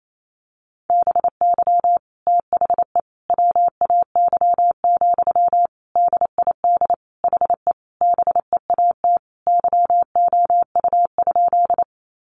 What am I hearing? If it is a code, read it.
Morse code, heard as a keyed tone, about 28 wpm, 700 Hz. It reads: BY THE WAY, DID HE BEAT YOU?
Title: Sentences from Top 1300 Words 28wpm